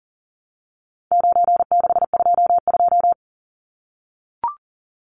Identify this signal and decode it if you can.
Morse code, heard as a keyed tone, about 40 wpm, 700 Hz. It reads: 9622
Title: Numbers - 4-Digit 40wpm